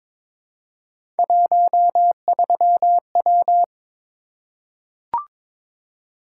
Morse code, heard as a keyed tone, about 22 wpm, 700 Hz. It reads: 13W
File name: practice / 22wpm / Three Letter-Number Combinations - 22wpm